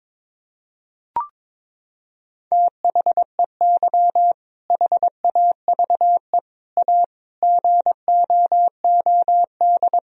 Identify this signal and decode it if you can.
Morse code, heard as a keyed tone, about 22 wpm, 700 Hz. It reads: THEY HAVE A GOOD
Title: Sentences from Top 1000 Words - Review Part 4 - 22wpm